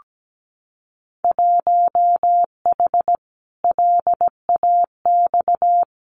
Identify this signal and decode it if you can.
Morse code, heard as a keyed tone, about 17 wpm, 700 Hz. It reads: 1H LAX
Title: Winter Field Day Exchanges 17wpm